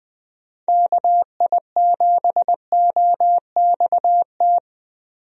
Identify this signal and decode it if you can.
Morse code, heard as a keyed tone, about 20 wpm, 700 Hz. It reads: KI7OXT
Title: USA Call Signs - 6 Characters - Repeated 2x - 20wpm